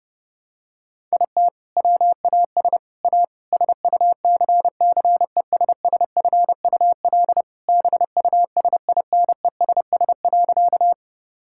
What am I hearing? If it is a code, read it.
Morse code, heard as a keyed tone, about 30 wpm, 700 Hz. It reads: IT WAS A SUCCESSFUL BUSINESS.
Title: Sentences from Top 1000 Words - Review Part 6 - 30wpm